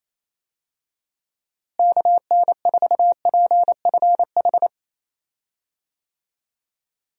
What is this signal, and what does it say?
Morse code, heard as a keyed tone, about 28 wpm, 700 Hz. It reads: KN4PFH
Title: USA Call Signs - 6 Characters - Repeated 3x - 28wpm